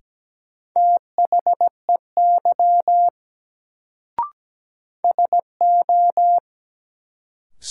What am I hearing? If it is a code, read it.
Morse code, heard as a keyed tone, about 17 wpm, 700 Hz. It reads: THEY SO
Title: Sets of 2 Words - New Word Drill - 17wpm